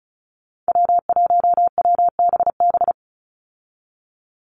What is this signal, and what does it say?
Morse code, heard as a keyed tone, about 34 wpm, 700 Hz. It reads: W1WBB
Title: Call Signs Repeated 2x - 35wpm